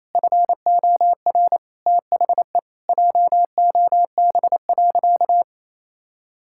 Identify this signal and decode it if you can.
Morse code, heard as a keyed tone, about 28 wpm, 700 Hz. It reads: FOR THE JOB.
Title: Sentences from Top 1600 Words 28wpm